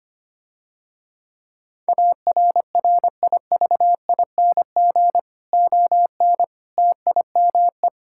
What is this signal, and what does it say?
Morse code, heard as a keyed tone, about 25 wpm, 700 Hz. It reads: ARRIVING ON TIME
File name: practice / 25wpm / Difficult Sentences with Hints - 25wpm